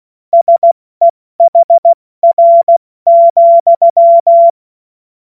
Morse code, heard as a keyed tone, about 16 wpm, 680 Hz. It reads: SEHR,
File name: Morsekurs / Witze / Witz 14. März 2023 70 F80